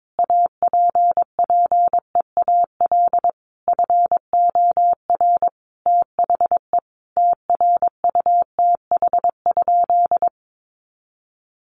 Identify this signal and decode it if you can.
Morse code, heard as a keyed tone, about 22 wpm, 700 Hz. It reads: APPEAL FOR THE TRUTH?
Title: Sentences from Top 1300 Words 22wpm